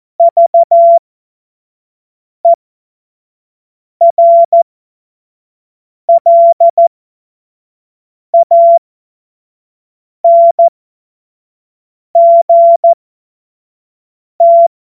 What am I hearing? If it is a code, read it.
Morse code, heard as a keyed tone, about 14 wpm, spaced as for 5 wpm, 680 Hz. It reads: VERLANGT